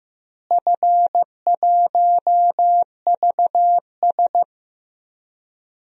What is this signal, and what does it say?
Morse code, heard as a keyed tone, about 15 wpm, 700 Hz. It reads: F1VS
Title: DXCC-100 Call Signs - 4 Characters - Repeated 2x - 15wpm